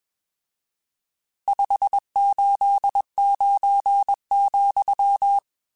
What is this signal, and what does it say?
Morse code, heard as a keyed tone, about 22 wpm, 800 Hz. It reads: 589,
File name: RSTs / RST-mp3s / Various RST @20